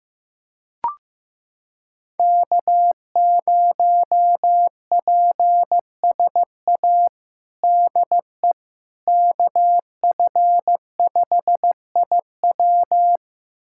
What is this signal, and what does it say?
Morse code, heard as a keyed tone, about 15 wpm, 700 Hz. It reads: K0PSA DE KF5IW
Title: USA Callsign DE Callsign - 5 Characters - 15wpm